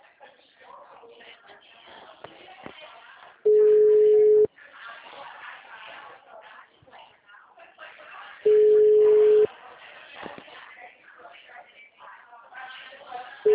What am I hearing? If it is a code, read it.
Ringback tone (Europe, 425 Hz), with speech.